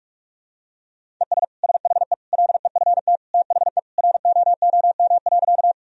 Morse code, heard as a keyed tone, about 46 wpm, 700 Hz. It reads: ES SHE LEFT THE ROOM.
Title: Sentences from Top 300 Words and Abbreviations 45wpm